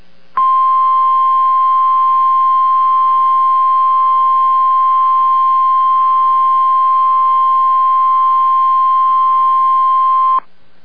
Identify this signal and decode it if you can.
Emergency Alert System (SAME) tones (the 1050 Hz attention tone). It is heard from 0.3 to 10.4 s.